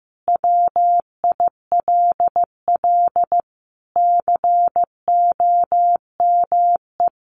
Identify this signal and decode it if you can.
Morse code, heard as a keyed tone, about 15 wpm, 700 Hz. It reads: WILL COME
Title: Sentences from Top 1500 Words 15wpm